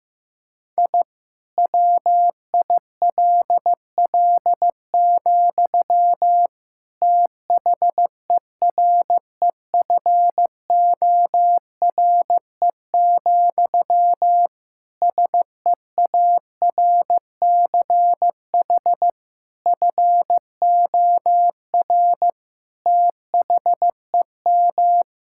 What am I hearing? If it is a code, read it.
Morse code, heard as a keyed tone, about 16 wpm, 700 Hz. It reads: I WILL, THEREFORE, SEARCH FOR THEM
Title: Sentences from Top 1000 Words - Review Part 5 - 15wpm